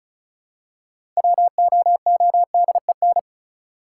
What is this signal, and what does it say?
Morse code, heard as a keyed tone, about 36 wpm, 700 Hz. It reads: WOODEN
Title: Six Letter Words - 35wpm